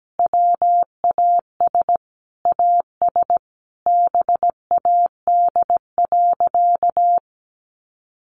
Morse code, heard as a keyed tone, about 17 wpm, 700 Hz. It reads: WAS AS BAD.
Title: Sentences from Top 1200 Words 17wpm